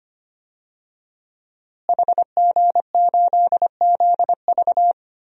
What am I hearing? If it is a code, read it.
Morse code, heard as a keyed tone, about 25 wpm, 700 Hz. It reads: HG8ZV
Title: DXCC-100 Call Signs - 5 Characters - Repeated 3x - 25wpm